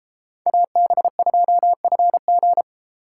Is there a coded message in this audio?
Yes: Morse code from 0.5 to 2.6 s, heard as a keyed tone, about 32 wpm, 700 Hz. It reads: AB2FG